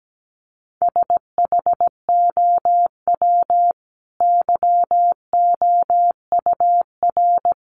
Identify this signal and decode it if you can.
Morse code, heard as a keyed tone, about 17 wpm, 700 Hz. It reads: SHOW YOUR
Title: Sentences from Top 1400 Words 17wpm